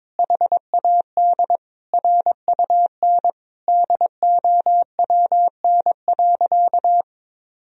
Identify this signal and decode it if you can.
Morse code, heard as a keyed tone, about 21 wpm, 700 Hz. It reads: HAD RUN DOWN.
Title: Sentences from Top 1000 Words - Review Part 2 - 22wpm